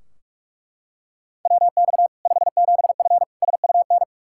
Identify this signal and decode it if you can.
Morse code, heard as a keyed tone, about 42 wpm, 700 Hz. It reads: WX 57F SUN
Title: K1USN SST QSO Practice 45wpm